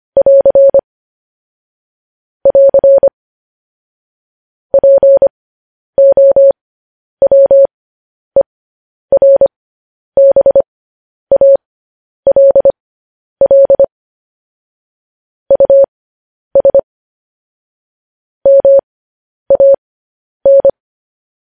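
Morse code, heard as a keyed tone, about 25 wpm, spaced as for 10 wpm, 550 Hz. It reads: <AR> <AR> POWERBALL US MAN
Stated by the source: Morse code news headlines practice transmission.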